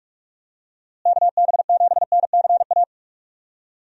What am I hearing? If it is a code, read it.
Morse code, heard as a keyed tone, about 46 wpm, 700 Hz. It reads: KB7NCA
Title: USA Call Signs - 6 Characters - Repeated 2x - 45wpm